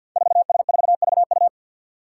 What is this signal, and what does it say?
Morse code, heard as a keyed tone, about 47 wpm, 700 Hz. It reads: 4S4VU
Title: Five Challenging Letter-Number Combinations 50wpm